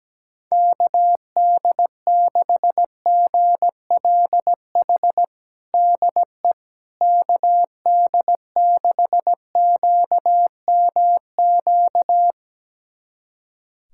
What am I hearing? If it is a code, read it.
Morse code, heard as a keyed tone, about 17 wpm, 700 Hz. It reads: KD6GLH DE KD6QMQ
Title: USA Callsign DE Callsign - 6 Characters - 17wpm